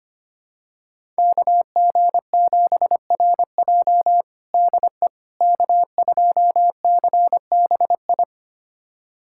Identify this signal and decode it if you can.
Morse code, heard as a keyed tone, about 25 wpm, 700 Hz. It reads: KG7RJ DE K2CBI
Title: USA Callsign DE Callsign - 5 Characters - 25wpm